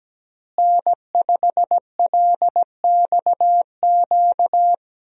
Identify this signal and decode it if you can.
Morse code, heard as a keyed tone, about 17 wpm, 700 Hz. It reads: N5LXQ
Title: USA Call Signs - 5 Characters - Repeated 2x - 17wpm